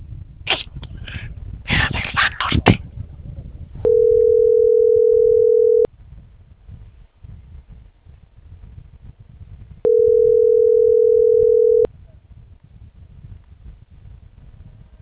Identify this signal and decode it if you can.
Ringback tone (North America), 440 + 480 Hz.